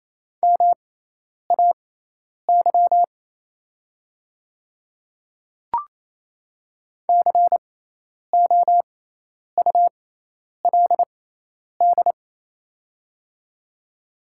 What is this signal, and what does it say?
Morse code, heard as a keyed tone, about 27 wpm, spaced as for 10 wpm, 700 Hz. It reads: MAY COULD
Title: 6x Character Spacing - Top 200 Words - 28wpm